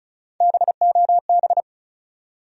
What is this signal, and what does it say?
Morse code, heard as a keyed tone, about 36 wpm, 700 Hz. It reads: BOB
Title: USA Common Names - Max 4 Letters - 35wpm